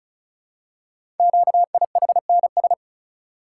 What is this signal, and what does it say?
Morse code, heard as a keyed tone, about 34 wpm, 700 Hz. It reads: QIHNS
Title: Code Groups - 5 Letters - 35wpm